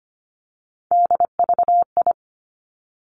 Morse code, heard as a keyed tone, about 25 wpm, 700 Hz. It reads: DVI
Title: Three Challenging Letter-Number Combinations 25wpm